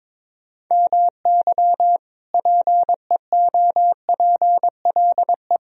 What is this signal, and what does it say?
Morse code, heard as a keyed tone, about 22 wpm, 700 Hz. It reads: MY PEOPLE
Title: Sentences from Top 1000 Words - Review Part 2 - 22wpm